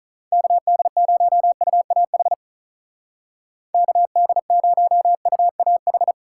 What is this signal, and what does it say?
Morse code, heard as a keyed tone, about 43 wpm, 700 Hz. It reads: KD0UAH KD0UAH
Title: Speed Racing - USA Call Signs 35wpm